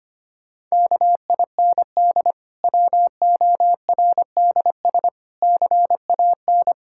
Morse code, heard as a keyed tone, about 25 wpm, 700 Hz. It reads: KIND WORDS CAN
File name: practice / 25wpm / Difficult Sentences with Hints - 25wpm